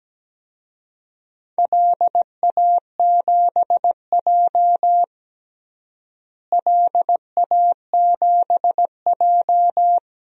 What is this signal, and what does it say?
Morse code, heard as a keyed tone, about 17 wpm, 700 Hz. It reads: LA7J LA7J
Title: DXCC-100 Call Signs - 4 Characters - Repeated 3x - 17wpm